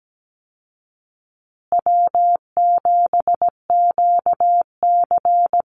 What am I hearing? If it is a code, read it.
Morse code, heard as a keyed tone, about 17 wpm, 700 Hz. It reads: W7QC
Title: Easy Call Signs Repeated 3x - 17wpm